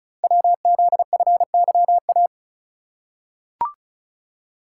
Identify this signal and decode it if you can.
Morse code, heard as a keyed tone, about 36 wpm, 690 Hz. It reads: WZFYA
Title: Code Groups - 5 Letters - 35wpm